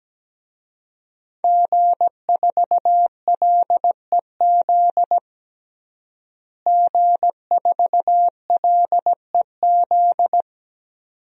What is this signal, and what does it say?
Morse code, heard as a keyed tone, about 17 wpm, 700 Hz. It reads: G4LEZ G4LEZ
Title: DXCC-100 Call Signs - 5 Characters - Repeated 2x - 17wpm